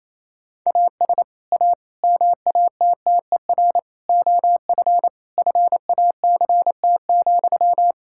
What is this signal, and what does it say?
Morse code, heard as a keyed tone, about 28 wpm, 700 Hz. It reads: AS A MATTER OF FACT,
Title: Sentences from Top 1600 Words 28wpm